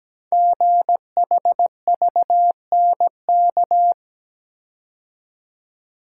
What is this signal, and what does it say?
Morse code, heard as a keyed tone, about 17 wpm, 700 Hz. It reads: GHVNK
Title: Code Groups - 5 Letters - 17wpm